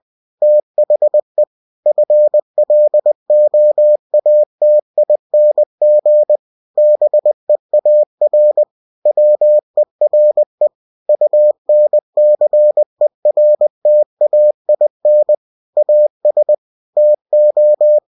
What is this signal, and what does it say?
Morse code, heard as a keyed tone, about 20 wpm, 600 Hz. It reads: THE FLOATING BEAR WERE UNCERTAIN AS TO
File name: cw-books / winnie-the-pooh / winnie-20wpm-page-0095